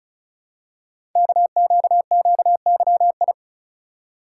Morse code, heard as a keyed tone, about 36 wpm, 700 Hz. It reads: KQQYI
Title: Code Groups - 5 Letter:Numbers - 35wpm